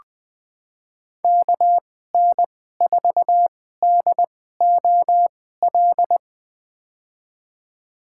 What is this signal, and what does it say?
Morse code, heard as a keyed tone, about 21 wpm, spaced as for 15 wpm, 700 Hz. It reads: KN4DOL
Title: USA Call Signs - 2x Character Spacing - 20wpm